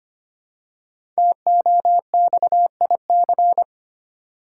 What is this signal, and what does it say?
Morse code, heard as a keyed tone, about 25 wpm, 700 Hz. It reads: TOXIC
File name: 5 Letter Words - First 3 Letters Given - 25wpm